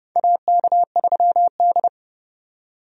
Morse code, heard as a keyed tone, about 32 wpm, 700 Hz. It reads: AK3D
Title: USA Call Signs - 4 Characters - 30wpm